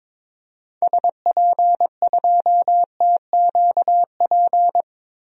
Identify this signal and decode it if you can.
Morse code, heard as a keyed tone, about 22 wpm, 700 Hz. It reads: SP2TQP
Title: DXCC-100 Call Signs - 6 Characters - 22wpm